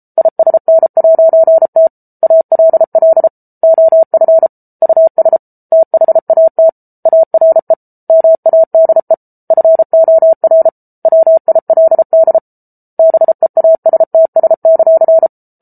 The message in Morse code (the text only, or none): ISN'T ALL OF US THAT ARE MADE FOR WILD BEASTS;